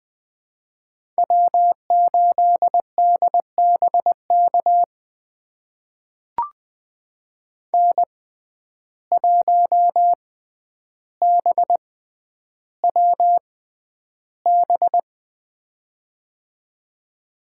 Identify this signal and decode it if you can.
Morse code, heard as a keyed tone, about 20 wpm, spaced as for 10 wpm, 700 Hz. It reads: W8DBK N1BWB
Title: USA Call Signs - 5 Characters - 6x Character Spacing - 20wpm